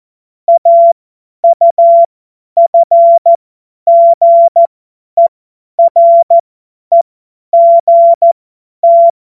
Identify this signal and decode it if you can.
Morse code, heard as a keyed tone, about 14 wpm, spaced as for 10 wpm, 680 Hz. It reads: AUFGEREGT
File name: Morsekurs / Witze / Witz 29 April 2025 50 F70